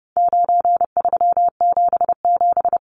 Morse code, heard as a keyed tone, about 30 wpm, 700 Hz. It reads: 9377
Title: Numbers - 4-Digit 30wpm